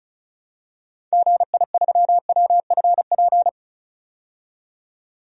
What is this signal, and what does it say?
Morse code, heard as a keyed tone, about 36 wpm, 700 Hz. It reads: GI3WFP
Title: DXCC-100 Call Signs - 6 Characters - 35wpm